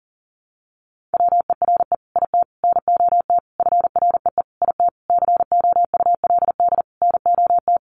Morse code, heard as a keyed tone, about 37 wpm, 700 Hz. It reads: WERE IT NOT FREE IT COULD NOT
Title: Sentences from Top 900 Words 40wpm